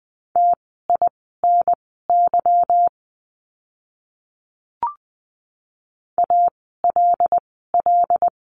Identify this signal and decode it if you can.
Morse code, heard as a keyed tone, about 20 wpm, spaced as for 15 wpm, 700 Hz. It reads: TINY ALL
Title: 146 Lesson 24f - Letter y - Words 20wpm